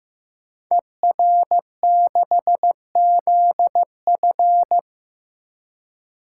Morse code, heard as a keyed tone, about 15 wpm, 700 Hz. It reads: ER6ZF
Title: Code Groups - 5 Letter:Numbers - 15wpm